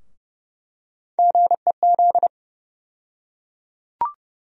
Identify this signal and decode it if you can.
Morse code, heard as a keyed tone, about 30 wpm, 700 Hz. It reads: GEZ
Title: USA Call Sign Postfixes - 3 Letters 30wpm